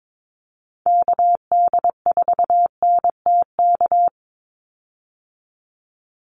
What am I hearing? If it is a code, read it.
Morse code, heard as a keyed tone, about 22 wpm, 700 Hz. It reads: KD4NTK
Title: USA Call Signs 22wpm